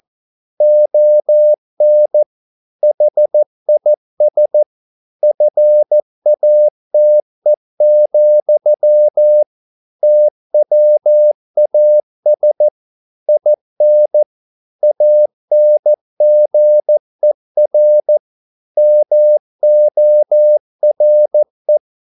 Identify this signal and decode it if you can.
Morse code, heard as a keyed tone, about 14 wpm, 600 Hz. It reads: ON HIS FATE, TWAS IN ANGER MORE
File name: mp3 files / a dogs mistake - 14wpm